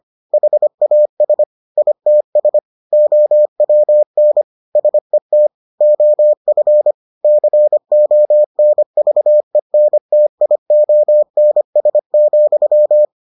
Morse code, heard as a keyed tone, about 25 wpm, 600 Hz. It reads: HAS ITS OWN SET OF CONVENTIONS,